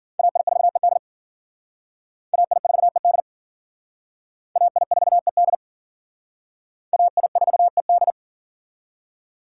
Morse code, heard as a keyed tone, about 51 wpm, 700 Hz. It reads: AI4ED AI4ED AI4ED AI4ED